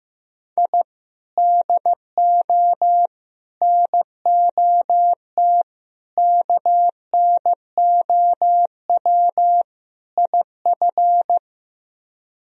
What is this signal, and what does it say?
Morse code, heard as a keyed tone, about 15 wpm, 700 Hz. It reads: I DO NOT KNOW IF
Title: Speed-Racing - Sets of 5 Words 15wpm